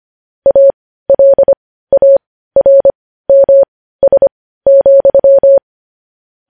Morse code, heard as a keyed tone, about 25 wpm, spaced as for 15 wpm, 550 Hz. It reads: ALARMS,
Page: news-headlines-in-morse-code-at-15-wpm